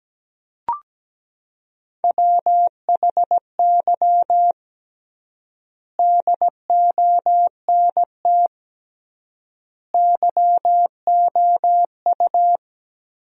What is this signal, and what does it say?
Morse code, heard as a keyed tone, about 17 wpm, 700 Hz. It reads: WHY DONT YOU
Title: 3x Word-Spacing - Sentences from Top 300 Words - 17wpm